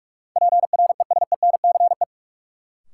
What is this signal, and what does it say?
Morse code, heard as a keyed tone, about 46 wpm, 700 Hz. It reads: PRESENCE
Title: Sets of 3 Words - New Word Drill - 45wpm